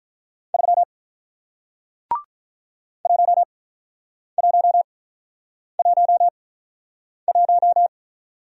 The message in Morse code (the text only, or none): 3 1111